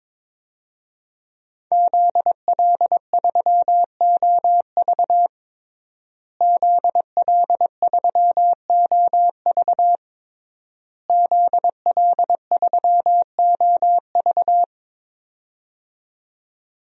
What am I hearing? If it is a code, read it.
Morse code, heard as a keyed tone, about 21 wpm, 700 Hz. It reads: ZL3OV ZL3OV ZL3OV
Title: DXCC-100 Call Signs - 5 Characters - Repeated 3x - 22wpm